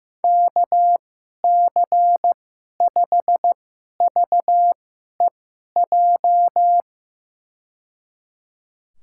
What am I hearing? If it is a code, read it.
Morse code, heard as a keyed tone, about 15 wpm, spaced as for 11 wpm, 700 Hz. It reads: KC5VEJ
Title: USA Call Signs - 6 Characters - 2x Character Spacing - 15wpm